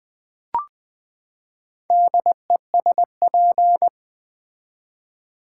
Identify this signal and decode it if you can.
Morse code, heard as a keyed tone, about 20 wpm, 700 Hz. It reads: DESP
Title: Build A Word - 4 Letters At A Time - 11-Letter Words - 20wpm